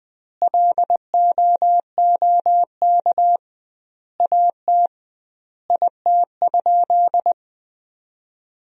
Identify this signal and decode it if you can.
Morse code, heard as a keyed tone, about 20 wpm, 700 Hz. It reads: LOOK AT IT?
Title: 2x Word-Spacing - Sentences from Top 100 Words - 20wpm